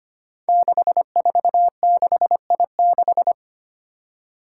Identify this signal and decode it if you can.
Morse code, heard as a keyed tone, about 25 wpm, 700 Hz. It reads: 646I6
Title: Five Challenging Letter-Number Combinations 25wpm